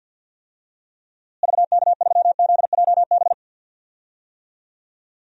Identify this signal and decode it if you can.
Morse code, heard as a keyed tone, about 50 wpm, 700 Hz. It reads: VK3ZJB
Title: DXCC-100 Call Signs - 6 Characters - Repeated 2x - 50wpm